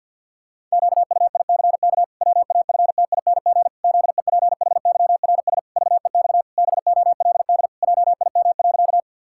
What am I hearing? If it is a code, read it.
Morse code, heard as a keyed tone, about 50 wpm, 700 Hz. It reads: QUICK WAFTING ZEPHYRS VEX BOLD JIM.
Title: Pangram Challenge v2 - 50wpm